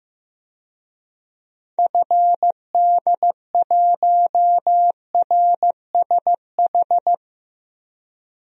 Morse code, heard as a keyed tone, about 15 wpm, 700 Hz. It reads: FD1RSH
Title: DXCC-100 Call Signs - 6 Characters - 15wpm